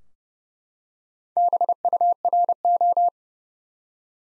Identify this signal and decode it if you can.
Morse code, heard as a keyed tone, about 30 wpm, 700 Hz. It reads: BURO
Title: Top 90 Abbr 30WPM